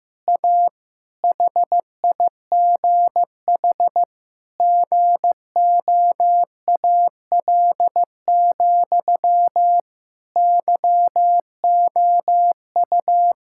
Morse code, heard as a keyed tone, about 15 wpm, 700 Hz. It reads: A HIGH GOAL, YOU
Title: Sentences from Top 1000 Words - Review Part 6 - 15wpm